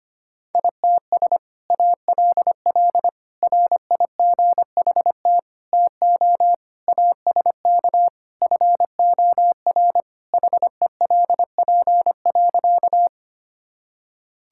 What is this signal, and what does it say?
Morse code, heard as a keyed tone, about 25 wpm, 700 Hz. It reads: ITS ALL RIGHT TO ASK FOR HELP.
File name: practice / 25wpm / Sentences with 2-Word Phrases - 25wpm